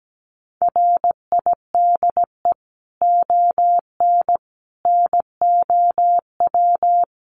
Morse code, heard as a keyed tone, about 17 wpm, 700 Hz. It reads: RIDE ON NOW
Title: Sentences from Top 1500 Words 17wpm